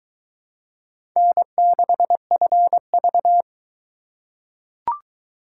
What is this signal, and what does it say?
Morse code, heard as a keyed tone, about 22 wpm, 700 Hz. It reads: N6FV